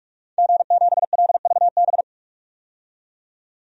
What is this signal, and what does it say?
Morse code, heard as a keyed tone, about 46 wpm, 700 Hz. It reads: G7LVB
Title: DXCC-100 Call Signs - 5 Characters - 45wpm